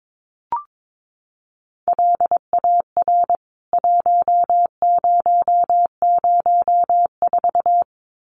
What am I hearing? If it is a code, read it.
Morse code, heard as a keyed tone, about 22 wpm, 700 Hz. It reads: LAR 1004
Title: CWT Exchange 22wpm